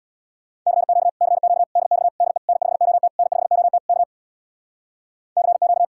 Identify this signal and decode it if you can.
Morse code, heard as a keyed tone, about 61 wpm, 700 Hz. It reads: CQ CQ DX DE N5OT N5OT K CQ
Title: Speed-Racing Common QSO Phrases Encore - 50wpm